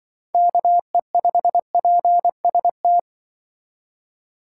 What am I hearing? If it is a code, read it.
Morse code, heard as a keyed tone, about 24 wpm, 700 Hz. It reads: KE5PST
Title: Speed Racing - USA Call Signs 20wpm